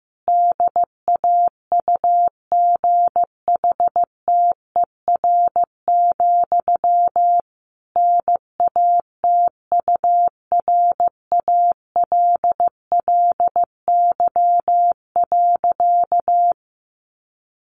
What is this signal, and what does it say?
Morse code, heard as a keyed tone, about 15 wpm, 700 Hz. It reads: DAUGHTER, NATURALLY.
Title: Sentences from Top 1500 Words 15wpm